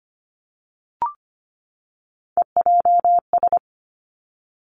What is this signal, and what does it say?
Morse code, heard as a keyed tone, about 25 wpm, 700 Hz. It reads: EJS
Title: Three Challenging Letter-Number Combinations 25wpm